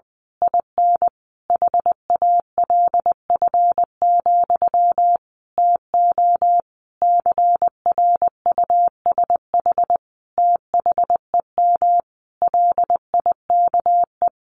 Morse code, heard as a keyed tone, about 20 wpm, 700 Hz. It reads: IN HALF, TO CRUSH THEM LIKE
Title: A Sound Of Thunder 20wpm